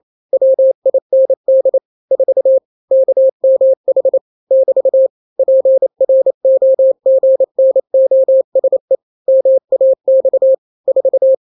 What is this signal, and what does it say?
Morse code, heard as a keyed tone, about 28 wpm, 530 Hz. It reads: WIND 4 KMH <BT> PROGNOSE MAX 4
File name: daily / 27wpm